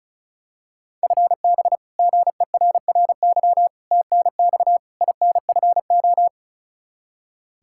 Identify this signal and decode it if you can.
Morse code, heard as a keyed tone, about 36 wpm, 700 Hz. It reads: FB GERRY TNX INFO
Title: Practice QSO - Exchange 3 - More Info - 35wpm